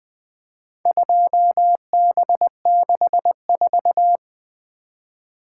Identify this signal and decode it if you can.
Morse code, heard as a keyed tone, about 20 wpm, 700 Hz. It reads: 2B64
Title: Four Letter-Number Combinations - 20wpm